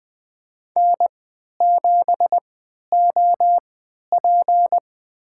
Morse code, heard as a keyed tone, about 20 wpm, spaced as for 11 wpm, 700 Hz. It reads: N7OP
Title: USA Call Signs - 4 Characters - 3x Character Spacing - 20wpm